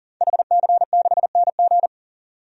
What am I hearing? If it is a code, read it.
Morse code, heard as a keyed tone, about 40 wpm, 700 Hz. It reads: HC6NG
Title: DXCC-100 Call Signs - 5 Characters - 40wpm